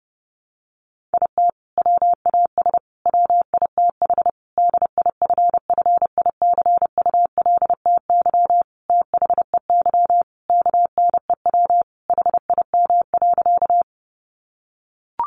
Morse code, heard as a keyed tone, about 28 wpm, 700 Hz. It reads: IT WAS WITH DIFFICULTY THEY KNEW HIM.
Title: Sentences from Top 1500 Words 30wpm